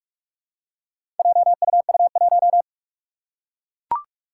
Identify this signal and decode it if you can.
Morse code, heard as a keyed tone, about 44 wpm, 700 Hz. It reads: JUU1
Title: Four Challenging Letter-Number Combinations 45wpm